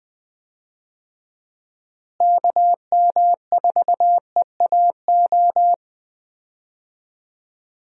Morse code, heard as a keyed tone, about 20 wpm, 700 Hz. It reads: KM4EAO
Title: USA Call Signs - 6 Characters - Repeated 3x - 20wpm